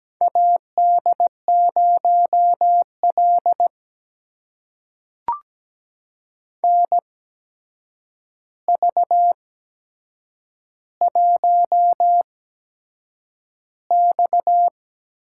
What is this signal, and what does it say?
Morse code, heard as a keyed tone, about 17 wpm, spaced as for 7 wpm, 700 Hz. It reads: AD0L NV1X
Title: USA Call Signs - 4 Characters - 8x Character Spacing - 17wpm